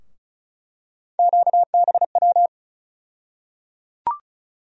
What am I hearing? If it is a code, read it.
Morse code, heard as a keyed tone, about 36 wpm, 700 Hz. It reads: QBW
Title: USA Call Sign Postfixes - 3 Letters 35wpm